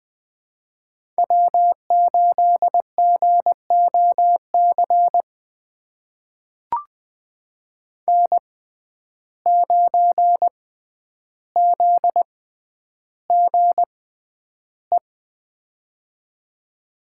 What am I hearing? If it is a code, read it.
Morse code, heard as a keyed tone, about 20 wpm, spaced as for 10 wpm, 700 Hz. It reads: W8GOC N9ZGE
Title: USA Call Signs - 6x Character Spacing - 20wpm